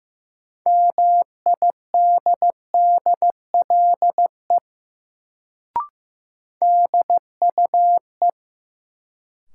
Morse code, heard as a keyed tone, about 15 wpm, 700 Hz. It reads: MIDDLE DUE
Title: Sentences from Top 900 Words - New Word Drill - 15wpm